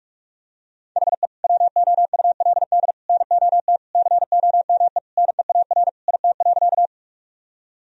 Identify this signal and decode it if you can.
Morse code, heard as a keyed tone, about 46 wpm, 700 Hz. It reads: HE WOULD NOT COME NEAR IT.
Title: Sentences from Top 1000 Words - Review Part 1 - 45wpm